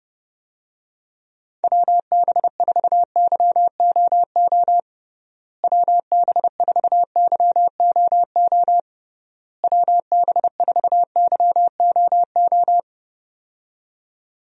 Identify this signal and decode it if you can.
Morse code, heard as a keyed tone, about 30 wpm, 700 Hz. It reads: WB4YOO WB4YOO WB4YOO
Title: USA Call Signs - 6 Characters - Repeated 3x - 30wpm